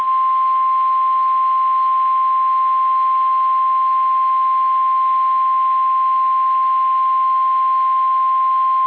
Emergency Alert System (SAME) tones (the 1050 Hz attention tone). It is heard from 0.4 to 8.5 s.
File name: eas / KSDZKDJL / KSDZKDJL